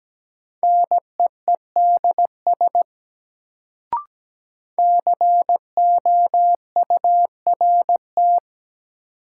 Morse code, heard as a keyed tone, about 17 wpm, 700 Hz. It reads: NEEDS COURT
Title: Sentences from Top 1600 Words - New Word Drill Alt - 17wpm